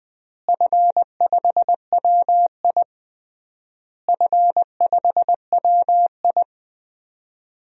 Morse code, heard as a keyed tone, about 20 wpm, 700 Hz. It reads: F5WI F5WI
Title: DXCC-100 Call Signs - 4 Characters - Repeated 3x - 20wpm